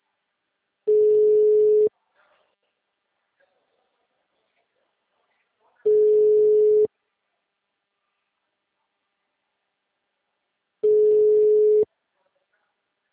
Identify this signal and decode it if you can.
Ringback tone (Europe, 425 Hz).
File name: out- -2007- - - .89120